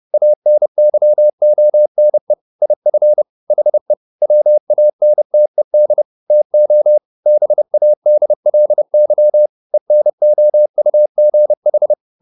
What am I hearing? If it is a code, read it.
Morse code, heard as a keyed tone, about 30 wpm, 600 Hz. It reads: ANYONE IF HE WANTED TO BADLY ENOUGH